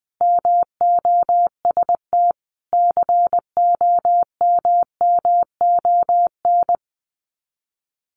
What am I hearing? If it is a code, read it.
Morse code, heard as a keyed tone, about 20 wpm, 700 Hz. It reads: MOST COMMON
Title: Sets of 5 Words Encore2 20wpm